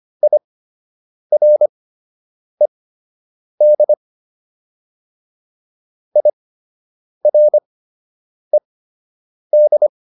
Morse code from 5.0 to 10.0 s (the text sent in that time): IRED